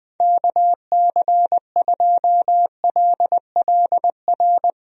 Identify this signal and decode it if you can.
Morse code, heard as a keyed tone, about 20 wpm, 700 Hz. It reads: KC2LLR
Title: USA Call Signs - 6 Characters - 20wpm